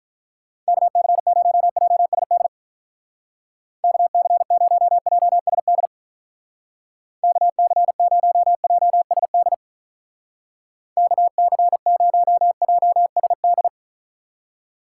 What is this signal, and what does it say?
Morse code, heard as a keyed tone, about 44 wpm, 700 Hz. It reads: KC0JSD KC0JSD KC0JSD KC0JSD